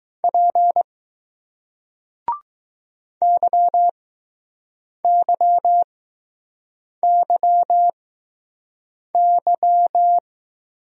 Morse code, heard as a keyed tone, about 20 wpm, spaced as for 6 wpm, 700 Hz. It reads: P YYYY